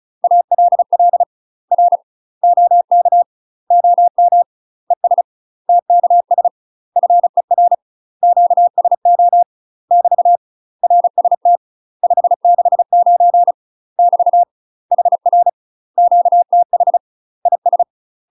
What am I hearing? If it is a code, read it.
Morse code, heard as a keyed tone, about 34 wpm, 700 Hz. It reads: ALL R OK OM ES TKS FER QSO <BT> RST 569 <BT> HR QTH IS
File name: UK MASTER QSOs TRAINING / 35WPM / UK MASTER QSOs 0042